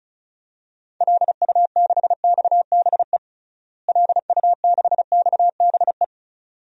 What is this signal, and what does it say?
Morse code, heard as a keyed tone, about 36 wpm, 700 Hz. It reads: LU6XBE LU6XBE
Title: DXCC-100 Call Signs - 6 Characters - Repeated 2x - 35wpm